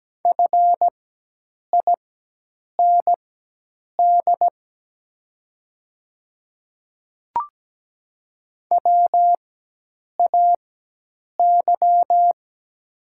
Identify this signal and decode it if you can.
Morse code, heard as a keyed tone, about 17 wpm, spaced as for 8 wpm, 700 Hz. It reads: FIND WAY
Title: 4x Character Spacing - Top 100 Words - 17wpm